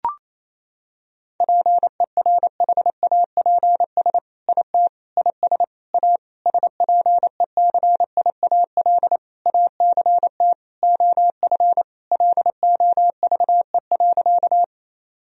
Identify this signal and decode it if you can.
Morse code, heard as a keyed tone, about 28 wpm, 700 Hz. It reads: PERHAPS IT IS A SPECIAL ACT OF LOVE.
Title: Sentences from Top 1000 Words - Review Part 3 - 28wpm